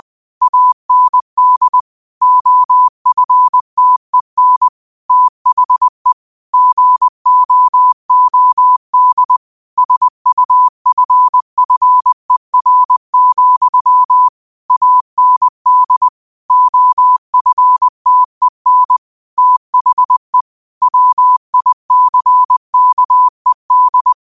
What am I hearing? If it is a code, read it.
Morse code, heard as a keyed tone, about 20 wpm, 1000 Hz. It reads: AND OFTEN THE GOOD SUFFER, AND OFTEN THE WICKED